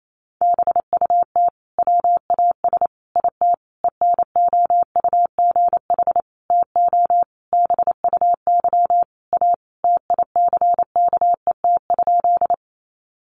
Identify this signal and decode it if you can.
Morse code, heard as a keyed tone, about 28 wpm, 700 Hz. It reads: BUT WAS IT ENOUGH TO BUY A TICKET?
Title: Sentences from Top 1400 Words 28wpm